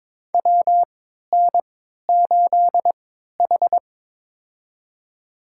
Morse code, heard as a keyed tone, about 22 wpm, spaced as for 13 wpm, 700 Hz. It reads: WN8H
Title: USA Call Signs - 4 Characters - 3x Character Spacing - 22wpm